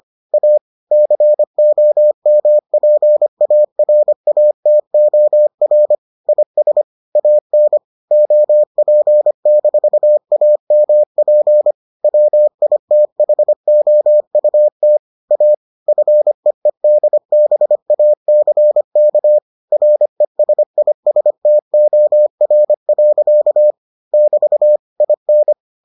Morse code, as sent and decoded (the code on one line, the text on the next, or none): .- / -.-. --- -- .--. .- .-. .- - --- .-. / .. ... / .- -. / --- .--. -....- .- -- .--. / .-- .. - .... --- ..- - / .- / ..-. . -.. -... .- -.-. -.- / .-. ... .. ... - --- .-. .-.-.- / -...- / .. -.
A COMPARATOR IS AN OP-AMP WITHOUT A FEEDBACK RESISTOR. <BT> IN